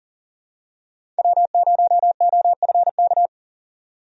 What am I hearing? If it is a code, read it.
Morse code, heard as a keyed tone, about 40 wpm, 700 Hz. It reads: W0OFK
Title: USA Call Signs - 5 Characters - 40wpm